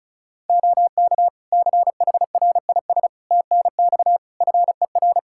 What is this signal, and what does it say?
Morse code, heard as a keyed tone, about 36 wpm, 700 Hz. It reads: OK CHRIS TNX FER
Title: Practice QSO - Exchange 4 - Ending - 35wpm